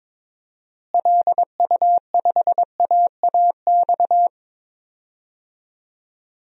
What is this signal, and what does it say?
Morse code, heard as a keyed tone, about 22 wpm, 700 Hz. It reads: LU5AAX
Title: DXCC-100 Call Signs - 6 Characters - Repeated 3x - 22wpm